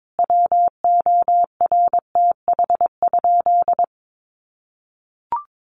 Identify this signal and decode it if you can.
Morse code, heard as a keyed tone, about 22 wpm, 700 Hz. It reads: WORTH?
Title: Sentences from Top 800 Words 22wpm